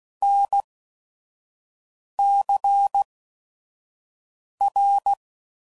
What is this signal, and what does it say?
Morse code, heard as a keyed tone, about 16 wpm, 800 Hz. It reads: N C R